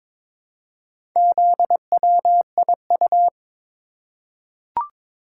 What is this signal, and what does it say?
Morse code, heard as a keyed tone, about 22 wpm, 700 Hz. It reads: ZWIU
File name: practice / 22wpm / Four Letter-Number Combinations - 22wpm